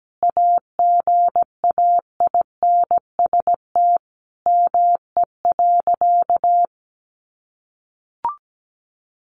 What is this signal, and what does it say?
Morse code, heard as a keyed tone, about 17 wpm, 700 Hz. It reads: AGAINST ME.
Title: Sentences from Top 1400 Words 17wpm